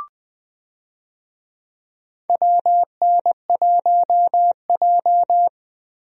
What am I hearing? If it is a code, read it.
Morse code, heard as a keyed tone, about 20 wpm, 700 Hz. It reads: WN1J
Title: USA Call Signs - 4 Characters - 20wpm